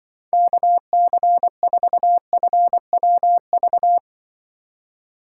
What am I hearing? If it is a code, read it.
Morse code, heard as a keyed tone, about 24 wpm, 700 Hz. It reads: KC4FWV
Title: Speed Racing - USA Call Signs 20wpm